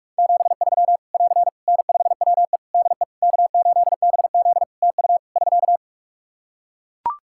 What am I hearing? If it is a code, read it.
Morse code, heard as a keyed tone, about 43 wpm, 700 Hz. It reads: 73 <AR> N5WE DE K8BZ TU <SK>
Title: K1USN SST QSO Practice 45wpm